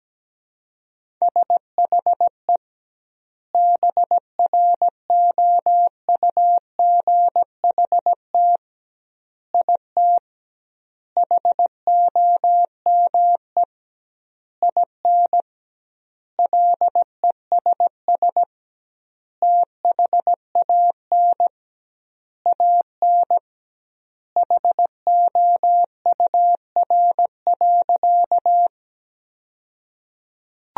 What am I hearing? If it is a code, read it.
Morse code, heard as a keyed tone, about 17 wpm, 700 Hz. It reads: SHE BROUGHT IT HOME IN LESS THAN AN HOUR.
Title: 2x Word-Spacing - Sentences from Top 500 Words - 17wpm